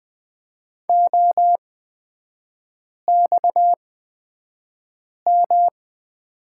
Morse code, heard as a keyed tone, about 20 wpm, spaced as for 5 wpm, 690 Hz. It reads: OXM
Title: 229 Lesson 37e - Letter x - Three Characters 20wpm